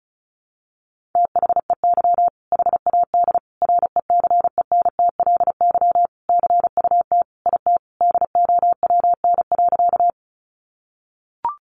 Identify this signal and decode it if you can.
Morse code, heard as a keyed tone, about 34 wpm, 700 Hz. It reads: THEY HAD RECENTLY CUT IT DOWN.
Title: Sentences from Top 1200 Words 35wpm